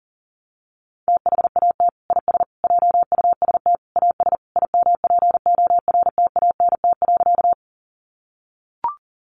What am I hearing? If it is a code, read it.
Morse code, heard as a keyed tone, about 37 wpm, 700 Hz. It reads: THAT IS JUST AS IMPORTANT.
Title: Sentences from Top 600 Words 40wpm